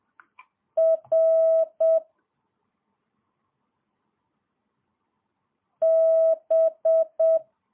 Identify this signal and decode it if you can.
Morse code, heard as a keyed tone, about 7 wpm, 650 Hz. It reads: R B
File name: jh8cbh / 2022cw